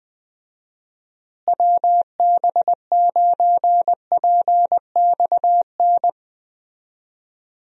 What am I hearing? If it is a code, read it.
Morse code, heard as a keyed tone, about 20 wpm, 700 Hz. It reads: WB9PXN
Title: USA Call Signs - 6 Characters - Repeated 3x - 20wpm